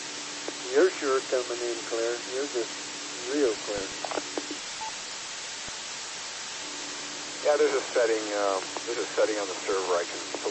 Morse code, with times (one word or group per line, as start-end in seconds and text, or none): none